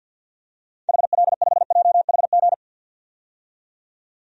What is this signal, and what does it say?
Morse code, heard as a keyed tone, about 53 wpm, 700 Hz. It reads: HL5JHG
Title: DXCC-100 Call Signs - 6 Characters - 50wpm